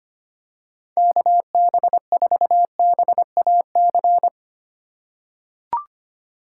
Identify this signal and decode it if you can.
Morse code, heard as a keyed tone, about 25 wpm, 700 Hz. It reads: KB4BAC
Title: USA Call Signs - 6 Characters - 2x Character Spacing - 25wpm